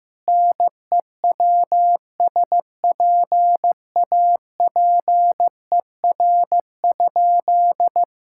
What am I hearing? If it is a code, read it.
Morse code, heard as a keyed tone, about 16 wpm, 700 Hz. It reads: NEWSPAPER?
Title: Sentences from Top 1000 Words - Review Part 6 - 15wpm